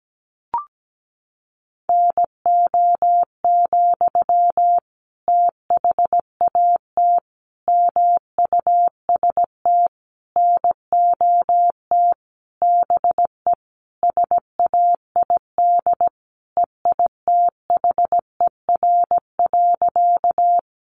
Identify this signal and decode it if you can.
Morse code, heard as a keyed tone, about 17 wpm, 700 Hz. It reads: NO, THAT MUST NOT BE SAID EITHER.
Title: Sentences from Top 600 Words 17wpm